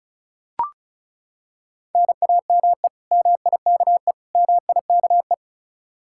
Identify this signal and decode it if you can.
Morse code, heard as a keyed tone, about 36 wpm, 700 Hz. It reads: NAME MIKE MIKE
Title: Practice QSO - Exchange 1 - Beginning - 35wpm